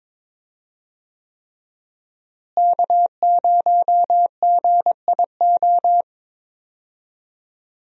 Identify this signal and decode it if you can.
Morse code, heard as a keyed tone, about 22 wpm, 700 Hz. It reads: K0GIO
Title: USA Call Signs - 5 Characters - Repeated 3x - 22wpm